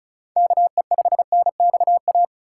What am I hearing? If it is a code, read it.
Morse code, heard as a keyed tone, about 34 wpm, 700 Hz. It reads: KE5NXA
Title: USA Call Signs - 2x Character Spacing - 35wpm